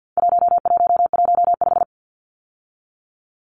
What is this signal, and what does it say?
Morse code, heard as a keyed tone, about 50 wpm, 700 Hz. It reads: 1115
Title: Numbers - 4-Digit 50wpm